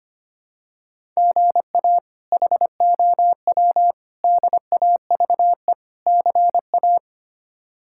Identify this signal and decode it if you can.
Morse code, heard as a keyed tone, about 25 wpm, 700 Hz. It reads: GA HOW DAVE CA
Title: K1USN SST QSO Practice - 25wpm